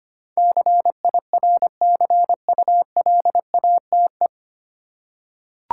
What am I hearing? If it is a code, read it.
Morse code, heard as a keyed tone, about 25 wpm, 690 Hz. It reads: CIRCULATE
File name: practice / 25wpm / 9 Letter Words - First 3 Letters Given - 25wpm